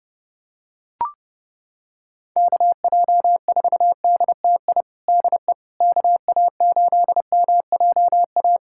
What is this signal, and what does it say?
Morse code, heard as a keyed tone, about 30 wpm, 700 Hz. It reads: KJ4DTI DE KA8MJA
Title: USA Callsign DE Callsign 30wpm